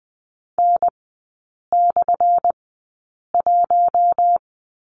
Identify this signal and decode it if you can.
Morse code, heard as a keyed tone, about 20 wpm, spaced as for 8 wpm, 700 Hz. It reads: N/1
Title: 178 Lesson 29e - Slash - Three Characters 20wpm